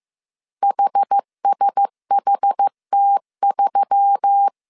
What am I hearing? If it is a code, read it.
Morse code, heard as a keyed tone, about 15 wpm, 800 Hz. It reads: HSHT3
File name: CW / DET / 1 6 1500C S A 14wpm 2921